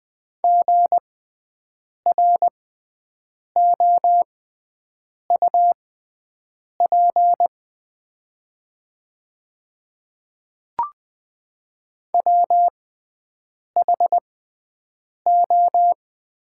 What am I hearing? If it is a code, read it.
Morse code, heard as a keyed tone, about 20 wpm, spaced as for 7 wpm, 700 Hz. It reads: GROUP WHO
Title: 6x Character Spacing - Top 500 Words - 20wpm